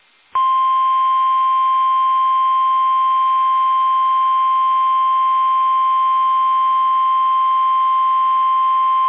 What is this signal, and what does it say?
Emergency Alert System (SAME) tones (the 1050 Hz attention tone). It is heard from 0.7 to 8.7 s.
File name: eas / KUSO / KUSO